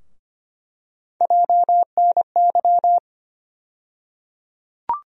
Morse code, heard as a keyed tone, about 25 wpm, 700 Hz. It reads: JNY
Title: USA Call Sign Postfixes - 3 Letters 25wpm